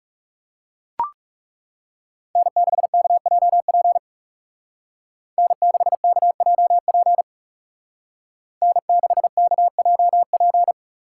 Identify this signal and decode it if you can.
Morse code, heard as a keyed tone, about 34 wpm, 700 Hz. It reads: N6KJP N6KJP N6KJP
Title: Speed Racing - USA 1x3 Call Signs 30wpm